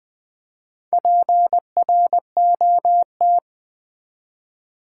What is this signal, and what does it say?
Morse code, heard as a keyed tone, about 20 wpm, 700 Hz. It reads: PROT
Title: Build A Word - 4 Letters At A Time - 10-Letter Words - 20wpm